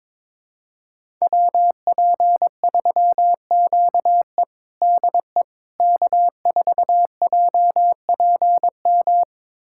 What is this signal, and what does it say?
Morse code, heard as a keyed tone, about 22 wpm, 700 Hz. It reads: WP3QE DE K4JPM
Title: USA Callsign DE Callsign - 5 Characters - 22wpm